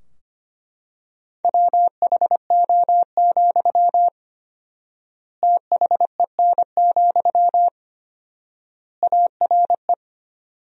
Morse code, heard as a keyed tone, about 25 wpm, 700 Hz. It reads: WHO, THEN, ARE
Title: 4x Word-Spacing - Sentences from Top 100 Words - 25wpm